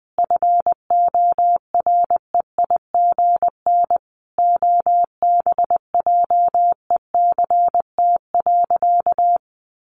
Morse code, heard as a keyed tone, about 20 wpm, 700 Hz. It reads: FOREIGN OBJECT.
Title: Sentences from Top 1400 Words 20wpm